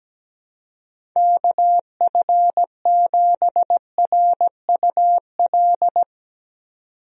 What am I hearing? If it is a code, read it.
Morse code, heard as a keyed tone, about 17 wpm, 700 Hz. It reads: KF7RUL
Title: USA Call Signs - 6 Characters - 17wpm